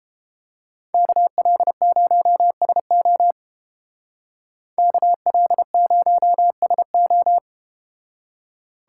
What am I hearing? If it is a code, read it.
Morse code, heard as a keyed tone, about 29 wpm, 700 Hz. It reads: KL0SO KL0SO